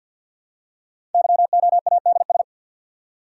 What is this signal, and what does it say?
Morse code, heard as a keyed tone, about 49 wpm, 700 Hz. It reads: YOADS
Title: Code Groups - 5 Letter:Numbers - 50wpm